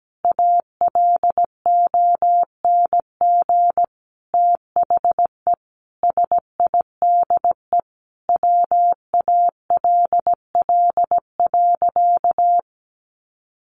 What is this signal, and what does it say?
Morse code, heard as a keyed tone, about 17 wpm, 700 Hz. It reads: ALONG THE SIDE WALL.
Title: Sentences from Top 1100 Words 17wpm